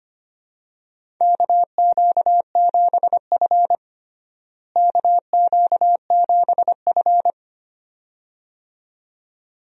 Morse code, heard as a keyed tone, about 25 wpm, 700 Hz. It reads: KQ7F KQ7F
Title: USA Call Signs - 4 Characters - Repeated 2x - 25wpm